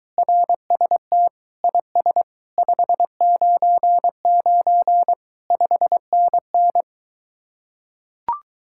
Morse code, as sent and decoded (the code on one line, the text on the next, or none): .-. ... - / .. ... / ..... ----. ----. / ..... -. -.
RST IS 599 5NN